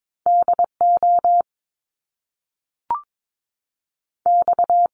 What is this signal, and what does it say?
Morse code, heard as a keyed tone, about 22 wpm, spaced as for 3 wpm, 700 Hz. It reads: D OX
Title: Top Two Letter Combinations in Words 22wpm